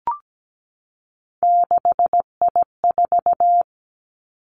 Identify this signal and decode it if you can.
Morse code, heard as a keyed tone, about 17 wpm, 700 Hz. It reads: 6I4
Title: Three Challenging Letter-Number Combinations 17wpm